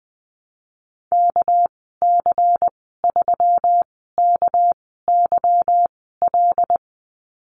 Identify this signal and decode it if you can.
Morse code, heard as a keyed tone, about 20 wpm, spaced as for 15 wpm, 700 Hz. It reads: KC3KYL
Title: 231 Lesson 37g - Letter x - Call Signs 20wpm